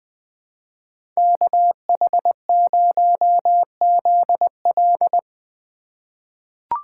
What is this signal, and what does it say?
Morse code, heard as a keyed tone, about 21 wpm, 700 Hz. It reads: KH0ZL
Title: USA Call Signs - 5 Characters - 2x Character Spacing - 20wpm